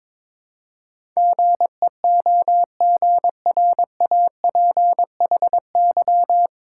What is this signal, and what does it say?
Morse code, heard as a keyed tone, about 21 wpm, 700 Hz. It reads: GEOGRAPHY
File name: 9 Letter Words - First 3 Letters Given - 22wpm